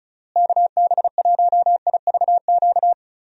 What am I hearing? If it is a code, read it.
Morse code, heard as a keyed tone, about 34 wpm, 700 Hz. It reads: KB1IVQ
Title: USA Call Signs - 6 Characters - Repeated 3x - 35wpm